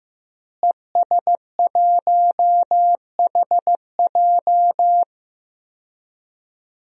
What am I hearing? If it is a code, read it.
Morse code, heard as a keyed tone, about 15 wpm, 700 Hz. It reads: ES1HJ
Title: DXCC-100 Call Signs - 5 Characters - Repeated 3x - 15wpm